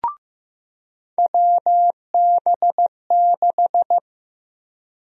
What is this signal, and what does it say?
Morse code, heard as a keyed tone, about 15 wpm, 700 Hz. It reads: WB6
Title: USA Call Sign Prefixes - 15wpm